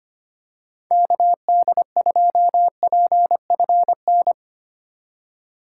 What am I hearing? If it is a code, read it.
Morse code, heard as a keyed tone, about 25 wpm, 700 Hz. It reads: KD2PFN
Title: USA Call Signs - 4x Character Spacing - 25wpm